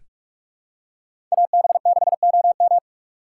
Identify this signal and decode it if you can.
Morse code, heard as a keyed tone, about 46 wpm, 700 Hz. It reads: AB6OM
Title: USA Call Signs - 3x Character Spacing - 45wpm